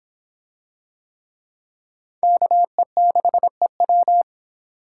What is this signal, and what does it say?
Morse code, heard as a keyed tone, about 25 wpm, 700 Hz. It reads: KE6EW